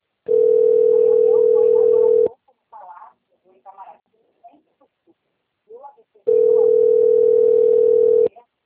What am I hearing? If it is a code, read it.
Ringback tone (North America), 440 + 480 Hz, with speech.
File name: rg-1600-unknown- - - .56353